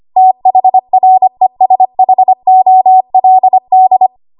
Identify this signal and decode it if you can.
Morse code, heard as a keyed tone, about 25 wpm, 750 Hz. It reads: THRESHOLD
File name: Archive / 25 WPM / 25WPM